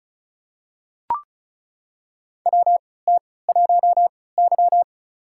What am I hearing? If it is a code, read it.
Morse code, heard as a keyed tone, about 36 wpm, spaced as for 20 wpm, 700 Hz. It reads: WT1Y
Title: USA Call Signs - 4 Characters - 3x Character Spacing - 35wpm